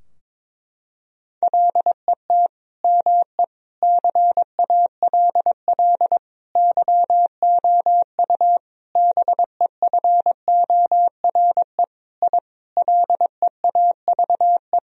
Morse code, heard as a keyed tone, about 22 wpm, 700 Hz. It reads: LET ME CALL YOU BEFORE I LEAVE
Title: Sentences from Top 1000 Words - Review Part 2 - 22wpm